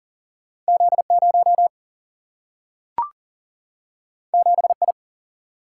Morse code, heard as a keyed tone, about 40 wpm, 700 Hz. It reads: Z0 7I
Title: Single-Letter Single-Number 40wpm